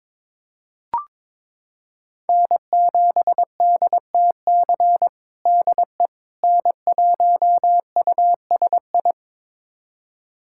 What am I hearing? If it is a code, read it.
Morse code, heard as a keyed tone, about 21 wpm, 700 Hz. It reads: N7DTC DE N1USI
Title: USA Callsign DE Callsign - 5 Characters - 22wpm